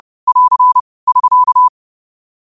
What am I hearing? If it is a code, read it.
Morse code, heard as a keyed tone, about 30 wpm, 1000 Hz. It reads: <AR> <SK>
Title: 30 WPM morse code quotes for Fri, 01 Mar 2024